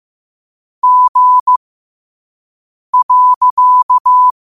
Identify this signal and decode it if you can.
Morse code, heard as a keyed tone, about 15 wpm, spaced as for 5 wpm, 1000 Hz. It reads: G.
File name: qotdfiles / 5wpm